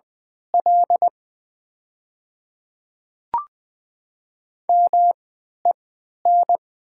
Morse code, heard as a keyed tone, about 20 wpm, spaced as for 11 wpm, 700 Hz. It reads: L MEN
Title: 3x Character Spacing - Top 500 Words - 20wpm